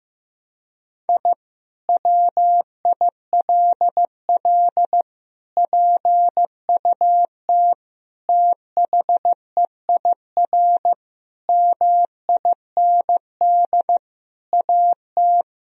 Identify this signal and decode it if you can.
Morse code, heard as a keyed tone, about 15 wpm, 700 Hz. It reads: I WILL PUT THEIR MIND AT
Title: Sentences from Top 1000 Words - Review Part 1 - 15wpm